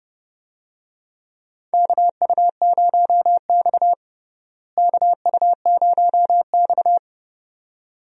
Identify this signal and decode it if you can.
Morse code, heard as a keyed tone, about 32 wpm, 700 Hz. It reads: KU0X KU0X
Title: USA Call Signs - 4 Characters - Repeated 2x - 30wpm